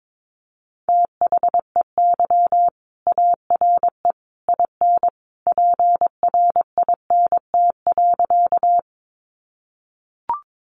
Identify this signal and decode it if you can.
Morse code, heard as a keyed tone, about 22 wpm, 700 Hz. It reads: THEY ARE IN PRINT.
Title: Sentences from Top 1400 Words 22wpm